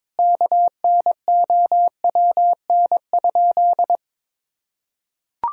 Morse code, heard as a keyed tone, about 22 wpm, 700 Hz. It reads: KNOWN?
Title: Sentences from Top 1000 Words - Review Part 3 - 22wpm